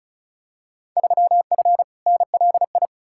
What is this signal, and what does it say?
Morse code, heard as a keyed tone, about 34 wpm, 700 Hz. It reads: 3F NLI
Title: Field Day Exchanges - 35wpm